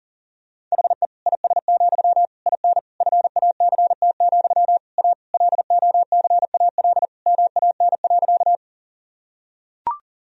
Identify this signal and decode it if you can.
Morse code, heard as a keyed tone, about 40 wpm, 700 Hz. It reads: HE IS, IN FACT, A LOCAL MAN.
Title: Sentences from Top 1000 Words - Review Part 3 - 40wpm